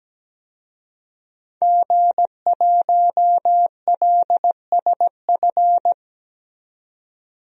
Morse code, heard as a keyed tone, about 17 wpm, 700 Hz. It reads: G1LSF
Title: DXCC-100 Call Signs - 5 Characters - 17wpm